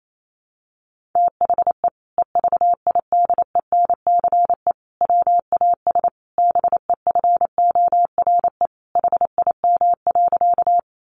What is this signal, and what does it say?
Morse code, heard as a keyed tone, about 28 wpm, 700 Hz. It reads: THE EVIDENCE WAS BEFORE HIM.
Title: Sentences from Top 1300 Words 28wpm